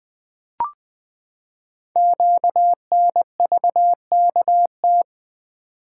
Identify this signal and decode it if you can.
Morse code, heard as a keyed tone, about 20 wpm, 700 Hz. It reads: QNVKT
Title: Code Groups - 5 Letter:Numbers - 20wpm